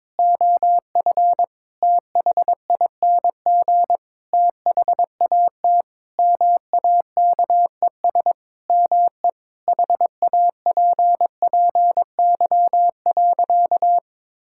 Morse code, heard as a keyed tone, about 22 wpm, 700 Hz. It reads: OF THING THAT MAKES ME HAPPY.
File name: Sentences with 7-Word Phrases - 22wpm